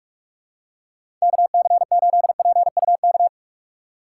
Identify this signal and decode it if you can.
Morse code, heard as a keyed tone, about 46 wpm, 700 Hz. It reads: KC8PUK
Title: USA Call Signs - 2x Character Spacing - 45wpm